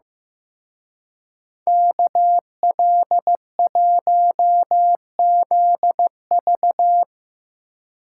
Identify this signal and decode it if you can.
Morse code, heard as a keyed tone, about 17 wpm, 700 Hz. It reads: KL1ZV
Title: Speed Racing - USA 2x2 Call Signs 15wpm